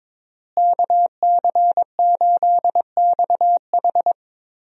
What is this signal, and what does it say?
Morse code, heard as a keyed tone, about 22 wpm, 700 Hz. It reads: KC8XH
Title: USA Call Signs - 5 Characters - Repeated 3x - 22wpm